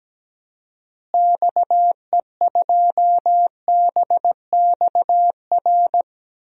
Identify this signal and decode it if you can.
Morse code, heard as a keyed tone, about 17 wpm, 700 Hz. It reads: XE2BXR
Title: DXCC-100 Call Signs - 6 Characters - Repeated 3x - 17wpm